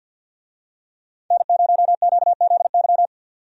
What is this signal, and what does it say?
Morse code, heard as a keyed tone, about 47 wpm, 700 Hz. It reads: N0QZY
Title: USA Call Signs - 5 Characters - 50wpm